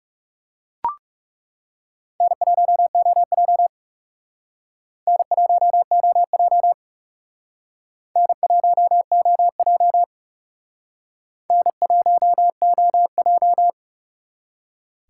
Morse code, heard as a keyed tone, about 37 wpm, 700 Hz. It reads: N1OJ N1OJ N1OJ N1OJ
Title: Speed Racing - USA 1x2 Call Signs 30wpm